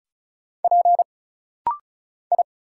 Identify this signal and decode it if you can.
Morse code, heard as a keyed tone, about 35 wpm, 700 Hz. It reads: P I
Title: Single Letter-Number Alt - Rapid-Fire 35wpm